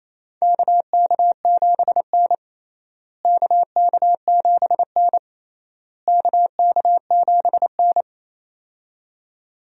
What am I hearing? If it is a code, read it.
Morse code, heard as a keyed tone, about 28 wpm, 700 Hz. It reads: KK7N KK7N KK7N
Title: USA Call Signs - 4 Characters - Repeated 3x - 28wpm